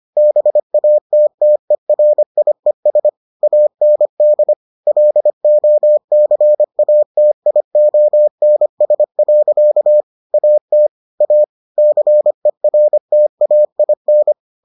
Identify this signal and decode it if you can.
Morse code, heard as a keyed tone, about 25 wpm, 600 Hz. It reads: BATTERIES AND LOCATIONS. AT A CERTAIN